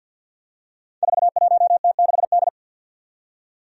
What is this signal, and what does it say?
Morse code, heard as a keyed tone, about 52 wpm, 700 Hz. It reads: 41T6D
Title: Five Challenging Letter-Number Combinations 50wpm